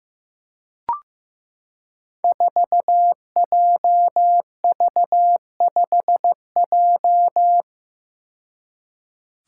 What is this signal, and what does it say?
Morse code, heard as a keyed tone, about 15 wpm, 700 Hz. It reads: 4JV5J
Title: Five Challenging Letter-Number Combinations 15wpm